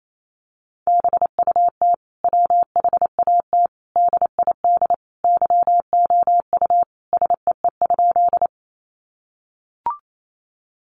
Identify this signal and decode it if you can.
Morse code, heard as a keyed tone, about 28 wpm, 700 Hz. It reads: BUT WHAT DID YOU SEE?
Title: Sentences from Top 100 Words 28wpm